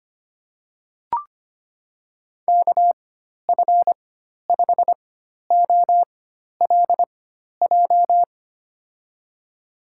Morse code, heard as a keyed tone, about 25 wpm, spaced as for 12 wpm, 700 Hz. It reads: KF5OLJ
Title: USA Call Signs - 6 Characters - 4x Character Spacing - 25wpm